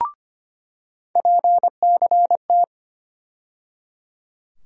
Morse code, heard as a keyed tone, about 25 wpm, 700 Hz. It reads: PCT
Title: USA Call Sign Postfixes - 3 Letters 25wpm